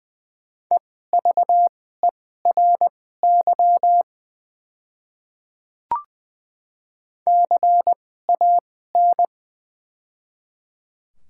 Morse code, heard as a keyed tone, about 21 wpm, spaced as for 15 wpm, 700 Hz. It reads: EVERY CAN
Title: 2x Character Spacing - Top 200 Words - 20wpm